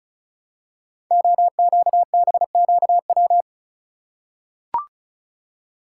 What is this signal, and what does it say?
Morse code, heard as a keyed tone, about 36 wpm, 700 Hz. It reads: OQBQW
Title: Code Groups - 5 Letters - 35wpm